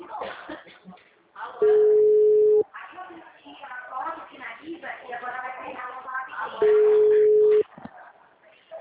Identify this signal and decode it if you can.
Ringback tone (Europe, 425 Hz), with speech.